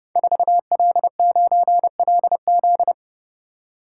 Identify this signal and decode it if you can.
Morse code, heard as a keyed tone, about 30 wpm, 700 Hz. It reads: 4L9LZ
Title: Code Groups - 5 Letter:Numbers - 30wpm